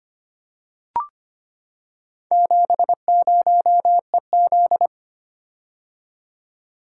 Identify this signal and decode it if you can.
Morse code, heard as a keyed tone, about 25 wpm, 690 Hz. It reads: 70EZ
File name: practice / 25wpm / Four Letter-Number Combinations - 25wpm